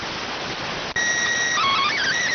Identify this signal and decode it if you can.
SSTV (slow-scan television), PD 290.